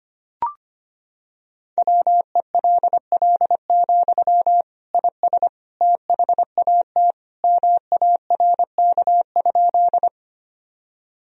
Morse code, heard as a keyed tone, about 25 wpm, 700 Hz. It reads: WELL, IS THAT MARK?
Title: Sentences from Top 1000 Words - Review Part 1 - 25wpm